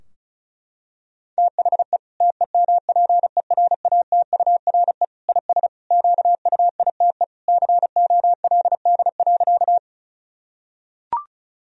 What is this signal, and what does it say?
Morse code, heard as a keyed tone, about 34 wpm, 700 Hz. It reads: THE TEMPERATURE IS QUITE COLD.
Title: Sentences from Top 1000 Words - Review Part 4 - 35wpm